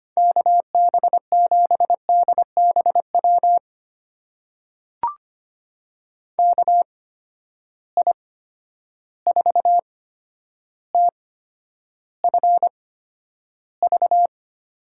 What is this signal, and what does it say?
Morse code, heard as a keyed tone, about 25 wpm, spaced as for 11 wpm, 700 Hz. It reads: KB7DBW KI4TFV
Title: USA Call Signs - 6 Characters - 8x Character Spacing - 25wpm